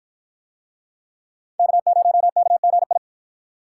Morse code, heard as a keyed tone, about 53 wpm, 700 Hz. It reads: K0KGI